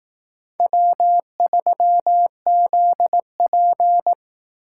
Morse code, heard as a keyed tone, about 18 wpm, 700 Hz. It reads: W3ZP
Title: Speed Racing - USA 1x2 Call Signs 15wpm